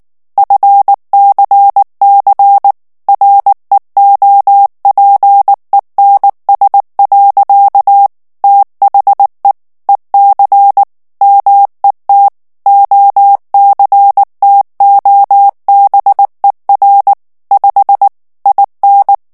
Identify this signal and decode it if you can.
Morse code, heard as a keyed tone, about 19 wpm, 800 Hz. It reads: FCC REOPENS. THE EC MET OCTOBER 5 IN
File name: ANGLAIS texte clair 3 vitesse 18 mots minute